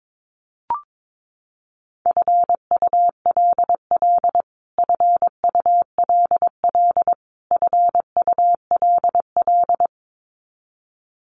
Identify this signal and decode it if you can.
Morse code, heard as a keyed tone, about 22 wpm, 700 Hz. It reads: FULL FULL FULL
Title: Top 500 Words Repeated 22wpm